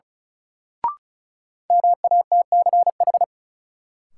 Morse code, heard as a keyed tone, about 34 wpm, 700 Hz. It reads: MATCH
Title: Sentences from Top 1600 Words - New Word Drill - 35wpm